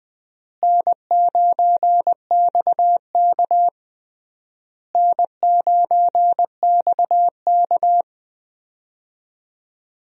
Morse code, heard as a keyed tone, about 20 wpm, 700 Hz. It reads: N9XK N9XK
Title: USA Call Signs - 4 Characters - Repeated 3x - 20wpm